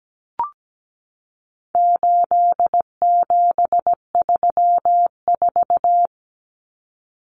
Morse code, heard as a keyed tone, about 17 wpm, 700 Hz. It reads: 8734
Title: Common Cut Numbers - 4-Digit 17wpm